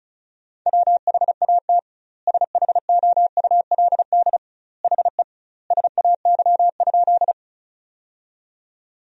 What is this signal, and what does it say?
Morse code, heard as a keyed tone, about 36 wpm, 700 Hz. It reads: WHAT SHOULD HE SAY?
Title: 2x Word-Spacing - Sentences from Top 300 Words - 35wpm